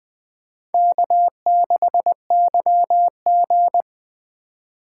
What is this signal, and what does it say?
Morse code, heard as a keyed tone, about 20 wpm, 700 Hz. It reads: K6YG